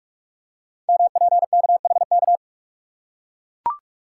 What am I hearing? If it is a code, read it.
Morse code, heard as a keyed tone, about 44 wpm, 700 Hz. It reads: MPKHK
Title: Code Groups - 5 Letter:Numbers - 45wpm